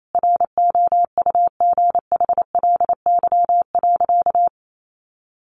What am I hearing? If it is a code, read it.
Morse code, heard as a keyed tone, about 28 wpm, 700 Hz. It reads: ROUGHLY.
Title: Sentences from Top 1500 Words 28wpm